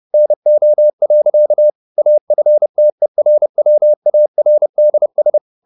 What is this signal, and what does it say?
Morse code, heard as a keyed tone, about 30 wpm, 600 Hz. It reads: NO. AFTERWARDS